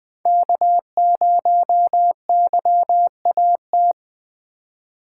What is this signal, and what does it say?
Morse code, heard as a keyed tone, about 20 wpm, 700 Hz. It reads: K0YAT
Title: USA Call Signs - 5 Characters - 3x Character Spacing - 20wpm